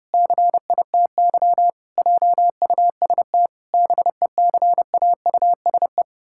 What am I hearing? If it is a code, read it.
Morse code, heard as a keyed tone, about 32 wpm, 700 Hz. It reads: CITY JUST BECAUSE
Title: Pangram Challenge v2 - 30wpm